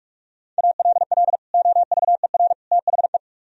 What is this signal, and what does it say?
Morse code, heard as a keyed tone, about 49 wpm, 700 Hz. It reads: ALL OVER THE